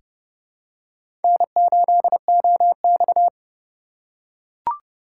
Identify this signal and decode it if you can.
Morse code, heard as a keyed tone, about 32 wpm, 700 Hz. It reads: N8OX
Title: USA Call Signs - 4 Characters - 2x Character Spacing - 30wpm